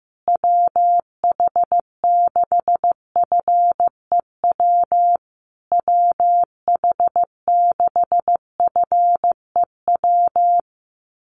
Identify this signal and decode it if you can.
Morse code, heard as a keyed tone, about 15 wpm, 700 Hz. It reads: WH6FEW WH6FEW
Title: USA Call Signs Repeated 3x 15wpm